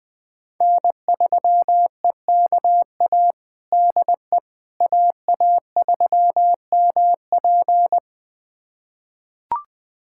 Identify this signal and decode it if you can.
Morse code, heard as a keyed tone, about 20 wpm, 700 Hz. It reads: N3EKA DE AA3MP
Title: USA Callsign DE Callsign - 5 Characters - 20wpm